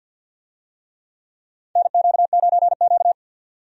Morse code, heard as a keyed tone, about 52 wpm, 700 Hz. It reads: NQ9Q
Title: USA Call Signs - 4 Characters - 50wpm